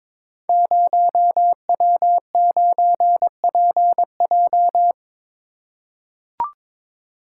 Morse code, heard as a keyed tone, about 23 wpm, 700 Hz. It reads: 0W9PJ
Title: Code Groups - 5 Letter:Numbers - 22wpm